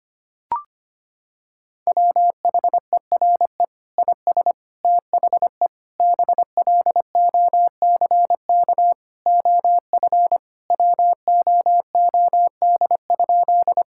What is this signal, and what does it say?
Morse code, heard as a keyed tone, about 25 wpm, 700 Hz. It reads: WHERE IS THE BLOCK OF WOOD?
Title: Sentences from Top 1600 Words 25wpm